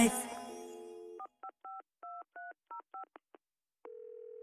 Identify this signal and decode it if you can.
DTMF, heard as telephone touch tones, tones of about 120 ms. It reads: 4552305